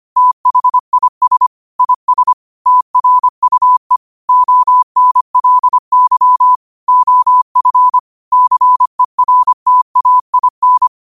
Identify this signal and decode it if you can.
Morse code, heard as a keyed tone, about 25 wpm, 1000 Hz. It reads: THIS IS TRUE ONLY OF CERTAIN
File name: qotdfiles / 25wpm